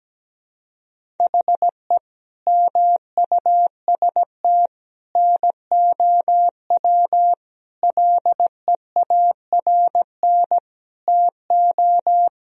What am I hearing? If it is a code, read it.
Morse code, heard as a keyed tone, about 17 wpm, 700 Hz. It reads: HE MUST NOW LEARN TO
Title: Sentences from Top 1000 Words - Review Part 1 - 17wpm